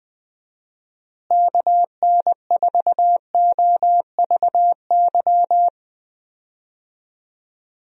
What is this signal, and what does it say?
Morse code, heard as a keyed tone, about 20 wpm, 700 Hz. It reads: KN4OVY
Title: USA Call Signs - 6 Characters - Repeated 3x - 20wpm